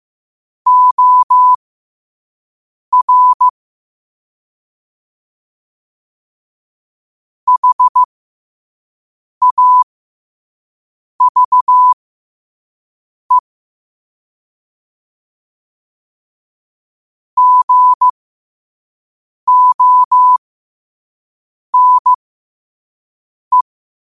Morse code, heard as a keyed tone, about 15 wpm, spaced as for 5 wpm, 1000 Hz. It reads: OR HAVE GONE
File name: qotdfiles / 5wpm